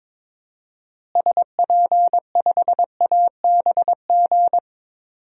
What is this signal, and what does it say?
Morse code, heard as a keyed tone, about 22 wpm, 700 Hz. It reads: SP5ABG
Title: DXCC-100 Call Signs - 6 Characters - Repeated 2x - 22wpm